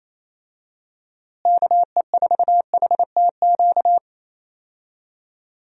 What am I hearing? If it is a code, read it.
Morse code, heard as a keyed tone, about 28 wpm, 700 Hz. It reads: KE4HTQ
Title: USA Call Signs - 6 Characters - Repeated 2x - 28wpm